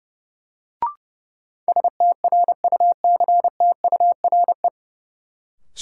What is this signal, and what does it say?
Morse code, heard as a keyed tone, about 31 wpm, 700 Hz. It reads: STRUCTURE
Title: Sentences from Top 1000 Words - Review Part 6 - New Word Drill - 30wpm